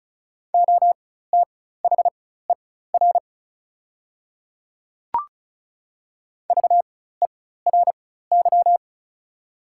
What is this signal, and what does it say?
Morse code, heard as a keyed tone, about 34 wpm, spaced as for 16 wpm, 690 Hz. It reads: OTHER VERY
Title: 4x Character Spacing - Top 200 Words - 35wpm